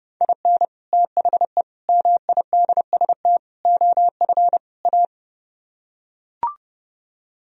Morse code, heard as a keyed tone, about 32 wpm, 700 Hz. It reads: IN THE MIDST OF A